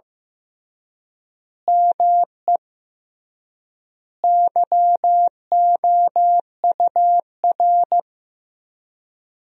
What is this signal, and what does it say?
Morse code, heard as a keyed tone, about 15 wpm, 700 Hz. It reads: ME YOUR
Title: 3x Word-Spacing - Sentences from Top 200 Words - 15wpm